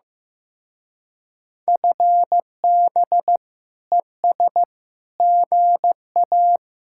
Morse code, heard as a keyed tone, about 18 wpm, 700 Hz. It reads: FB ES GA
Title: Speed-Racing Common QSO Phrases Encore - 15wpm